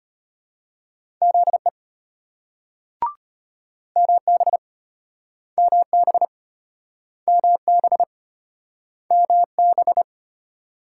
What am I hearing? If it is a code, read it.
Morse code, heard as a keyed tone, about 33 wpm, 700 Hz. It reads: ZE MB MB MB MB